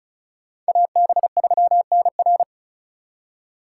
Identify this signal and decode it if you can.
Morse code, heard as a keyed tone, about 44 wpm, 700 Hz. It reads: AB3NR